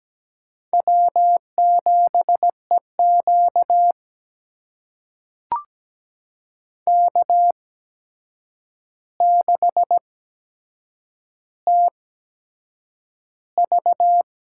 Morse code, heard as a keyed tone, about 17 wpm, spaced as for 7 wpm, 700 Hz. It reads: W7EQ K6TV
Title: USA Call Signs - 8x Character Spacing - 17wpm